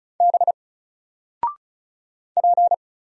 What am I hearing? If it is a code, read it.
Morse code, heard as a keyed tone, about 36 wpm, 700 Hz. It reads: B P
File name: Single-Letter-Number-35wpm